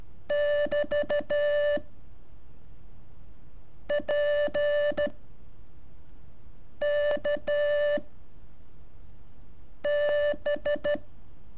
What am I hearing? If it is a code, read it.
Morse code, heard as a keyed tone, about 13 wpm, 580 Hz. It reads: <BT> P K B
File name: curso / ejercicio 18